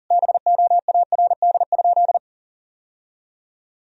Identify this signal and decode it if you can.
Morse code, heard as a keyed tone, about 40 wpm, 700 Hz. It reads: BOARD?
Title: Sentences from Top 1000 Words - Review Part 4 - 40wpm